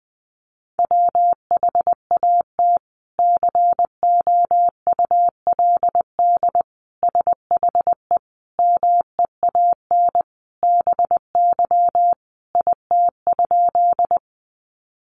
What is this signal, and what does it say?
Morse code, heard as a keyed tone, about 20 wpm, 700 Hz. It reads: WHAT COULD SHE MEAN BY IT?
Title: Sentences from Top 200 Words 20wpm